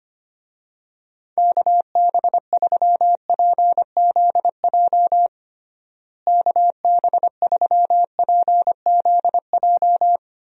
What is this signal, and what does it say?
Morse code, heard as a keyed tone, about 25 wpm, 700 Hz. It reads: KB3PZJ KB3PZJ
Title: USA Call Signs - 6 Characters - Repeated 2x - 25wpm